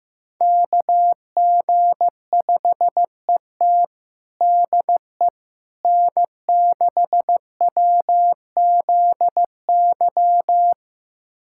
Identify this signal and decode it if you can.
Morse code, heard as a keyed tone, about 15 wpm, 700 Hz. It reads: KG5ET DE N6WZY
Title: USA Callsign DE Callsign - 5 Characters - 15wpm